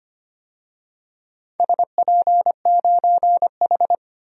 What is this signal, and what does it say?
Morse code, heard as a keyed tone, about 25 wpm, 700 Hz. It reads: SP9H
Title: DXCC-100 Call Signs - 4 Characters - 25wpm